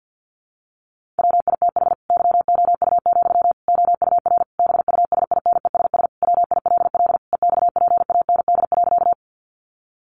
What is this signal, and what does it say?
Morse code, heard as a keyed tone, about 50 wpm, 700 Hz. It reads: WITH YOU, OUR BUSINESS WILL EXPAND.
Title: Sentences from Top 1400 Words 50wpm